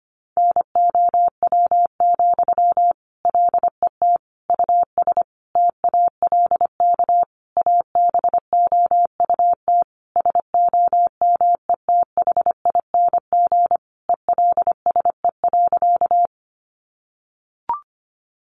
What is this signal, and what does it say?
Morse code, heard as a keyed tone, about 25 wpm, 700 Hz. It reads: NOW, LET US TALK ABOUT SOMETHING ELSE.
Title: Sentences from Top 800 Words 25wpm